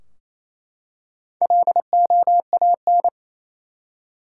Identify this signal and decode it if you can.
Morse code, heard as a keyed tone, about 28 wpm, 700 Hz. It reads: LOAN
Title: Four Letter Words - 28wpm